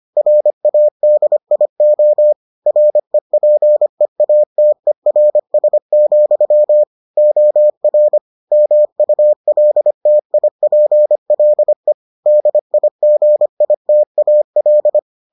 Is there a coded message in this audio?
Yes: Morse code from 0.2 to 15.0 s, heard as a keyed tone, about 25 wpm, 600 Hz. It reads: RADIO REPEATERS, OR MULTIPLE DIGITAL